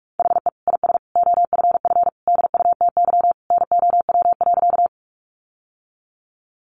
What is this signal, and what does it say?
Morse code, heard as a keyed tone, about 44 wpm, 700 Hz. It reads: HE IS OFF DUTY NOW.
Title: Sentences from Top 1500 Words 45wpm